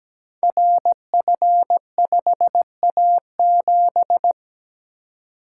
Morse code, heard as a keyed tone, about 17 wpm, 700 Hz. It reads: RF5A7
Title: Code Groups - 5 Letter:Numbers - 17wpm